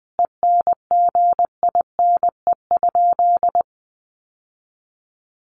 Morse code, heard as a keyed tone, about 20 wpm, 700 Hz. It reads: ENGINE?
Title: Sentences from Top 1500 Words 20wpm